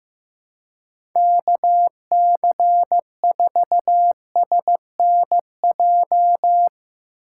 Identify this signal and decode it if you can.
Morse code, heard as a keyed tone, about 16 wpm, 700 Hz. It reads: KC4SNJ
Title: USA Call Signs - 6 Characters - 3x Character Spacing - 15wpm